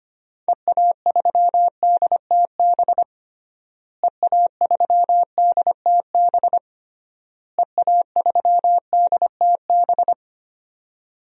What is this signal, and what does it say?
Morse code, heard as a keyed tone, about 25 wpm, 700 Hz. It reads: EA3DTB EA3DTB EA3DTB
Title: DXCC-100 Call Signs - 6 Characters - Repeated 3x - 25wpm